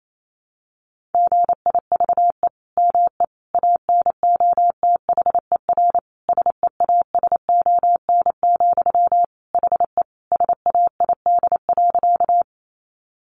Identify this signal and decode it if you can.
Morse code, heard as a keyed tone, about 28 wpm, 700 Hz. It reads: GIVE ME ANOTHER SEASON, HE SAID.
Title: Sentences from Top 1400 Words 28wpm